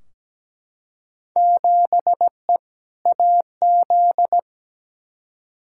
Morse code, heard as a keyed tone, about 17 wpm, 700 Hz. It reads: 7E AZ
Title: Field Day Exchanges Encore - 17wpm